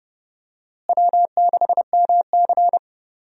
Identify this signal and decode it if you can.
Morse code, heard as a keyed tone, about 30 wpm, 700 Hz. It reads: W6MC
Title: USA Call Signs - 4 Characters - Repeated 2x - 30wpm